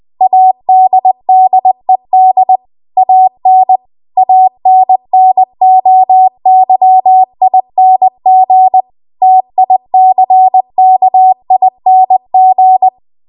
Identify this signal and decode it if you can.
Morse code, heard as a keyed tone, about 20 wpm, 750 Hz. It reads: ADDED AN ANNOYING TICKING
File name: Archive / 20 WPM / 20WPM